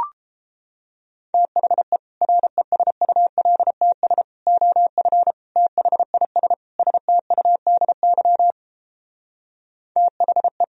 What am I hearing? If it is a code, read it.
Morse code, heard as a keyed tone, about 31 wpm, 700 Hz. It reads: THE RESULTS OF THIS STUDY THE